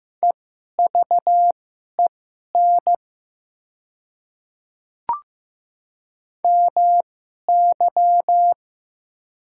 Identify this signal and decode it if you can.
Morse code, heard as a keyed tone, about 15 wpm, spaced as for 11 wpm, 700 Hz. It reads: EVEN MY
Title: 2x Character Spacing - Top 200 Words - 15wpm